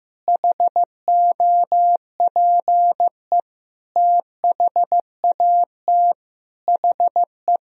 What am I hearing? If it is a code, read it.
Morse code, heard as a keyed tone, about 15 wpm, 700 Hz. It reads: HOPE THAT HE
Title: Sentences from Top 1000 Words - Review Part 1 - 15wpm